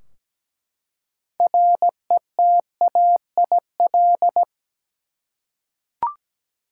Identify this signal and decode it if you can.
Morse code, heard as a keyed tone, about 17 wpm, 700 Hz. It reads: RETAIL
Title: Practice QSO - Exchange 3 - More Info - 17wpm